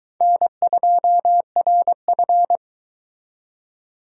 Morse code, heard as a keyed tone, about 23 wpm, 700 Hz. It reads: N2RF